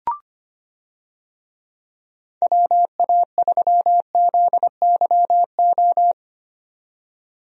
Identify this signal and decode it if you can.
Morse code, heard as a keyed tone, about 25 wpm, 700 Hz. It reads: WA3ZYO
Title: USA Call Signs - 6 Characters - 25wpm